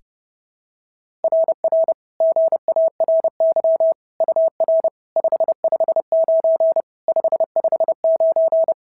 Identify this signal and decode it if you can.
Morse code, heard as a keyed tone, about 30 wpm, 690 Hz. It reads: RR GARY UR 559 559
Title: SOTA QSO Practice 30wpm